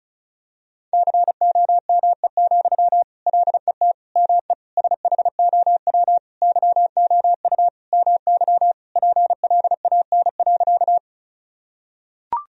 Morse code, heard as a keyed tone, about 34 wpm, 700 Hz. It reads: COME, LET ME SHOW YOU MY PLAN.
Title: Sentences from Top 1000 Words - Review Part 2 - 35wpm